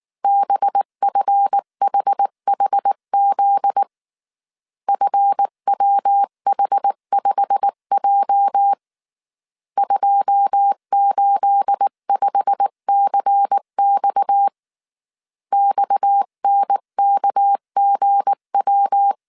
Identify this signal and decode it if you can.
Morse code, heard as a keyed tone, about 19 wpm, 800 Hz. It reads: BFHHZ FWH5J 285CX XNKGW